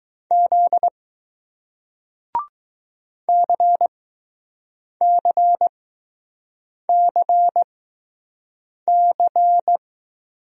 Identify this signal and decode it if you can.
Morse code, heard as a keyed tone, about 20 wpm, spaced as for 6 wpm, 700 Hz. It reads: Z CCCC